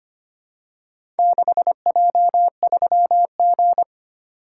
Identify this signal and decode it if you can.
Morse code, heard as a keyed tone, about 25 wpm, 690 Hz. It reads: 6J3G
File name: practice / 25wpm / Four Letter-Number Combinations - 25wpm